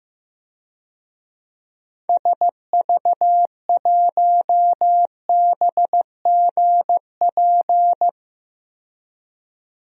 Morse code, heard as a keyed tone, about 15 wpm, 700 Hz. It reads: SV1BGP
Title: DXCC-100 Call Signs - 6 Characters - 15wpm